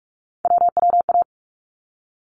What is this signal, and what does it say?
Morse code, heard as a keyed tone, about 44 wpm, 700 Hz. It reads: WWA
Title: ARRL Sections 45wpm